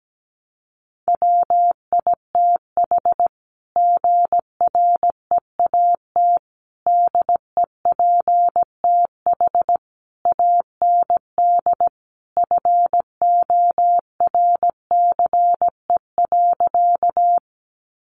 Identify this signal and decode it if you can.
Morse code, heard as a keyed tone, about 17 wpm, 700 Hz. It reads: WITH GREAT DEPTH AND FORCE.
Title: Sentences from Top 1200 Words 17wpm